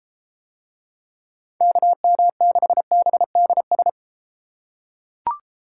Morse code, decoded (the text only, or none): KM6BDS